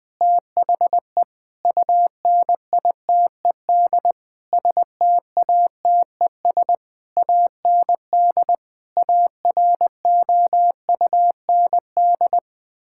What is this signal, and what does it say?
Morse code, heard as a keyed tone, about 20 wpm, 700 Hz. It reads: THE UNITED STATES AND AROUND
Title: Sentences with 8-Word Phrases - Spoken Beginning - 20wpm